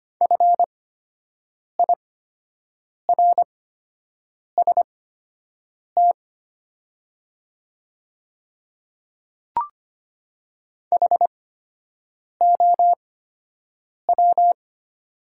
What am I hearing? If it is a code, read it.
Morse code, heard as a keyed tone, about 25 wpm, spaced as for 7 wpm, 700 Hz. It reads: FIRST HOW
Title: 8x Character Spacing - Top 100 Words - 25wpm